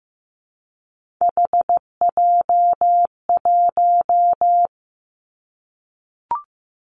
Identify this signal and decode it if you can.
Morse code, heard as a keyed tone, about 15 wpm, 700 Hz. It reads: HJ1
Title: Three Challenging Letter-Number Combinations 15wpm